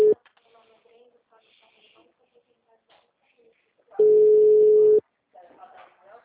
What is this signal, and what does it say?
Ringback tone (Europe, 425 Hz), with speech.